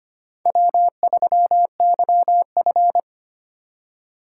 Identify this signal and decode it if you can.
Morse code, heard as a keyed tone, about 25 wpm, 700 Hz. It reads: W3YF
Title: USA Call Signs - 4 Characters - 6x Character Spacing - 25wpm